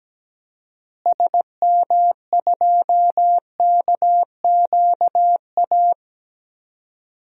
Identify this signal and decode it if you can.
Morse code, heard as a keyed tone, about 17 wpm, 700 Hz. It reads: SM2KQA
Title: DXCC-100 Call Signs - 6 Characters - Repeated 3x - 17wpm